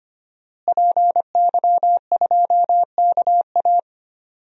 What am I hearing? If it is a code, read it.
Morse code, heard as a keyed tone, about 25 wpm, 700 Hz. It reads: PY2KA
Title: DXCC-100 Call Signs - 5 Characters - Repeated 2x - 25wpm